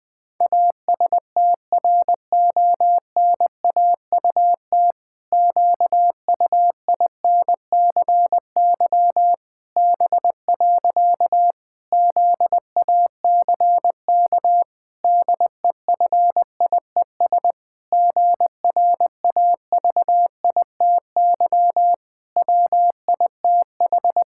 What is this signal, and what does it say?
Morse code, heard as a keyed tone, about 21 wpm, 700 Hz. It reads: ASTRONAUT QUINCY B. ZACK DEFIES GRAVITY WITH
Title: Pangram Challenge v2 - 20wpm